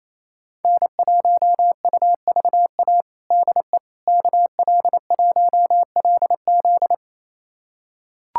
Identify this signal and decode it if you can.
Morse code, heard as a keyed tone, about 28 wpm, 700 Hz. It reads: N1UVA DE KL1LZ
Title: USA Callsign DE Callsign - 5 Characters - 28wpm